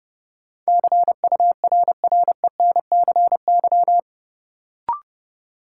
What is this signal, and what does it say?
Morse code, heard as a keyed tone, about 32 wpm, 700 Hz. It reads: CURRENCY
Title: Sentences from Top 1500 Words - New Word Drill - 30wpm